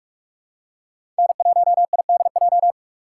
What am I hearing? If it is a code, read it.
Morse code, heard as a keyed tone, about 42 wpm, 700 Hz. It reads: N1IDJ
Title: USA Call Signs - 3x Character Spacing - 45wpm